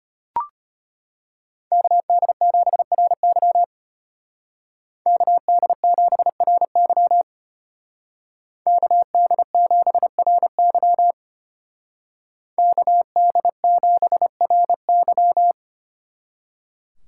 Morse code, heard as a keyed tone, about 30 wpm, 700 Hz. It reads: KD7RY KD7RY KD7RY KD7RY
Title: Speed Racing - USA 2x2 Call Signs 25wpm